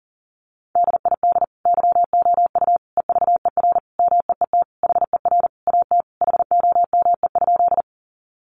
Morse code, heard as a keyed tone, about 40 wpm, 700 Hz. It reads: DID YOU EVER MEET HER AT HOME?
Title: Sentences from Top 1100 Words 40wpm